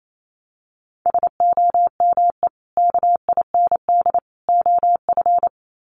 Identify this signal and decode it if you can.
Morse code, heard as a keyed tone, about 28 wpm, 700 Hz. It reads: SOME KIND OF
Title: Sets of 3 Words 28wpm